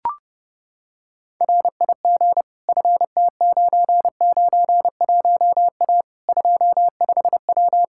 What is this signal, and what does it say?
Morse code, heard as a keyed tone, about 30 wpm, 700 Hz. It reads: RIG FT991A 25W
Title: Practice QSO - Exchange 2 - Info - 30wpm